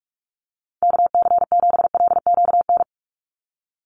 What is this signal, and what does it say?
Morse code, heard as a keyed tone, about 46 wpm, 700 Hz. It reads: KC7LQN
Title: USA Call Signs 45wpm